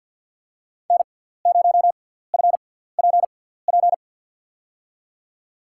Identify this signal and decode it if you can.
Morse code, heard as a keyed tone, about 53 wpm, spaced as for 17 wpm, 700 Hz. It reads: N0FPP
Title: USA Call Signs - 5 Characters - 6x Character Spacing - 50wpm